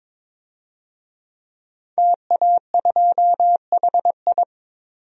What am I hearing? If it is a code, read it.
Morse code, heard as a keyed tone, about 21 wpm, 700 Hz. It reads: TA2HI
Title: DXCC-100 Call Signs - 5 Characters - 22wpm